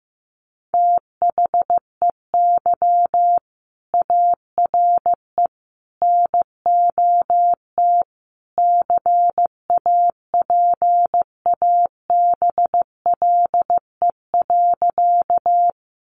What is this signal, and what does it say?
Morse code, heard as a keyed tone, about 15 wpm, 700 Hz. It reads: THEY ARE NOT CAPABLE.
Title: Sentences from Top 1200 Words 15wpm